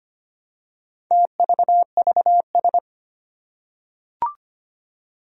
Morse code, heard as a keyed tone, about 25 wpm, 700 Hz. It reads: TVVS
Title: Four Challenging Letter-Number Combinations 25wpm